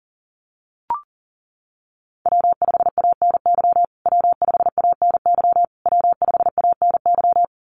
Morse code, heard as a keyed tone, about 40 wpm, 700 Hz. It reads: W5ANY W5ANY W5ANY
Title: USA Call Signs Repeated 3x 40wpm